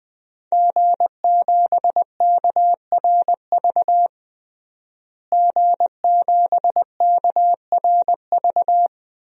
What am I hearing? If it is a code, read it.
Morse code, heard as a keyed tone, about 21 wpm, 700 Hz. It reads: G7KRV G7KRV
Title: DXCC-100 Call Signs - 5 Characters - Repeated 2x - 20wpm